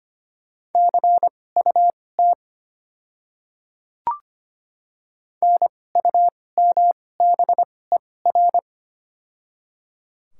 Morse code, heard as a keyed tone, about 25 wpm, spaced as for 18 wpm, 700 Hz. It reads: CUT NUMBER
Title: 2x Character Spacing - Top 500 Words - 25wpm